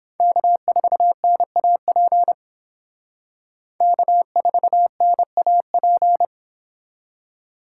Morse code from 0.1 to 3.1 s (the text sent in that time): K4NAP